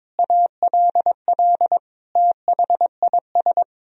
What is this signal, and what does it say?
Morse code, heard as a keyed tone, about 22 wpm, 700 Hz. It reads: ALL THIS
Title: Sentences from Top 1000 Words - Review Part 3 - 22wpm